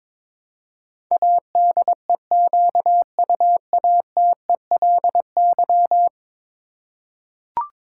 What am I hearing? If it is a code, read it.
Morse code, heard as a keyed tone, about 21 wpm, 700 Hz. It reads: ADEQUATELY
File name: practice / 22wpm / 10 Letter Words - First 3 Letters Given - 22wpm